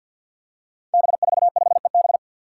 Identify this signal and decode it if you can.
Morse code, heard as a keyed tone, about 53 wpm, 700 Hz. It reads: B45EB
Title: Five Challenging Letter-Number Combinations 50wpm